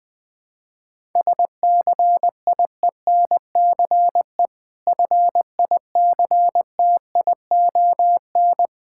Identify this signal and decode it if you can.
Morse code, heard as a keyed tone, about 20 wpm, 700 Hz. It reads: SCIENCE FICTION
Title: Difficult Words with Hints - 20wpm